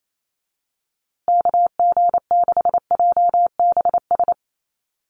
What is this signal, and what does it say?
Morse code, heard as a keyed tone, about 28 wpm, 700 Hz. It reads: KG6JBS
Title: USA Call Signs Repeated 3x 28wpm